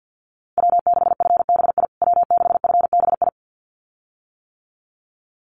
Morse code, heard as a keyed tone, about 50 wpm, 700 Hz. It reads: W6FBI W6FBI
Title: USA Call Signs Repeated 2x 50wpm